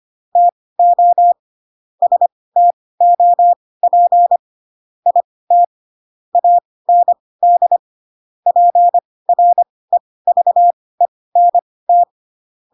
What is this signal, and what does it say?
Morse code, heard as a keyed tone, about 25 wpm, spaced as for 18 wpm, 700 Hz. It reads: TO STOP IT AND PREVENT
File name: Pinocchio ch3 12 25 ik1hge